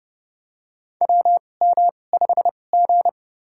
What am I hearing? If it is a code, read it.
Morse code, heard as a keyed tone, about 30 wpm, spaced as for 22 wpm, 700 Hz. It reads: WM5G
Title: USA Call Signs - 2x Character Spacing - 30wpm